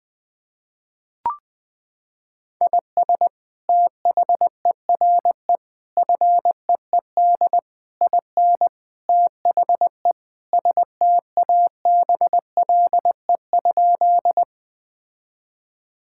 Morse code, heard as a keyed tone, about 20 wpm, 700 Hz. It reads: IS THERE FEED IN THE STABLE?
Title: Sentences from Top 1600 Words 20wpm